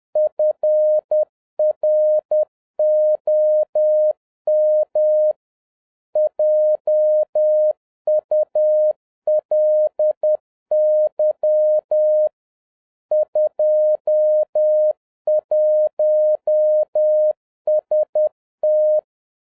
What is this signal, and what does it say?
Morse code, heard as a keyed tone, about 10 wpm, 610 Hz. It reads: FROM JULY 21ST